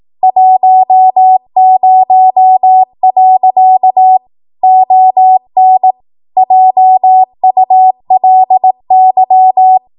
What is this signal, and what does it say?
Morse code, heard as a keyed tone, about 18 wpm, 750 Hz. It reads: 10. ON JULY
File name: Archive / 18 WPM / 18WPM